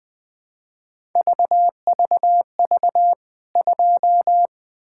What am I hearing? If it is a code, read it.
Morse code, heard as a keyed tone, about 20 wpm, 700 Hz. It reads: VVV 2
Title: K1USN SST QSO Practice 20wpm